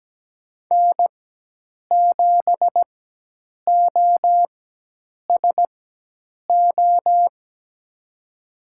Morse code, heard as a keyed tone, about 17 wpm, spaced as for 8 wpm, 700 Hz. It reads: N7OSO
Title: USA Call Signs - 5 Characters - 4x Character Spacing - 17wpm